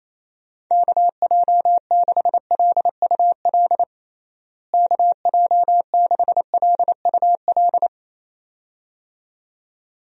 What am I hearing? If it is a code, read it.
Morse code, heard as a keyed tone, about 27 wpm, 700 Hz. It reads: KJ6LUL KJ6LUL
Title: USA Call Signs - 6 Characters - Repeated 2x - 28wpm